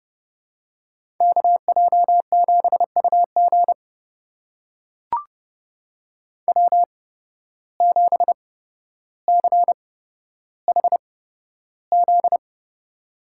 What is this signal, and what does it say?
Morse code, heard as a keyed tone, about 31 wpm, spaced as for 13 wpm, 700 Hz. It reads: KJ7UG W7CHZ
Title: USA Call Signs - 5 Characters - 8x Character Spacing - 30wpm